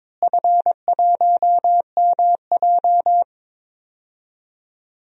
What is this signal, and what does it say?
Morse code, heard as a keyed tone, about 22 wpm, 700 Hz. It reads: F1MJ
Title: DXCC-100 Call Signs - 4 Characters - Repeated 3x - 22wpm